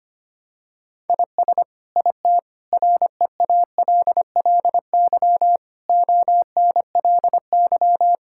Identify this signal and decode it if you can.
Morse code, heard as a keyed tone, about 25 wpm, 700 Hz. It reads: IS IT REALLY ONLY
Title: Sentences from Top 1000 Words - Review Part 6 - 25wpm